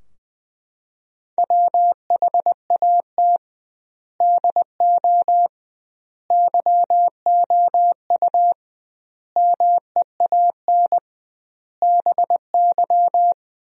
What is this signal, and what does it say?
Morse code, heard as a keyed tone, about 20 wpm, 700 Hz. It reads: WHAT DO YOU MEAN BY
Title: 2x Word-Spacing - Sentences from Top 200 Words - 20wpm